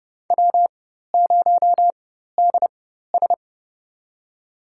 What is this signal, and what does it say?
Morse code, heard as a keyed tone, about 31 wpm, spaced as for 14 wpm, 700 Hz. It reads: W0DS
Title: USA Call Signs - 4 Characters - 4x Character Spacing - 30wpm